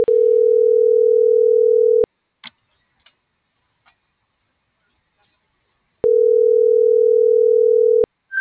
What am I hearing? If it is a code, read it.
Ringback tone (North America), 440 + 480 Hz.